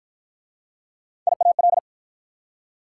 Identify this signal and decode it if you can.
Morse code, heard as a keyed tone, about 50 wpm, 690 Hz. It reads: IAL